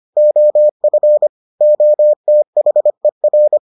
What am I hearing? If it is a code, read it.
Morse code, heard as a keyed tone, about 25 wpm, 600 Hz. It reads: OF OTHER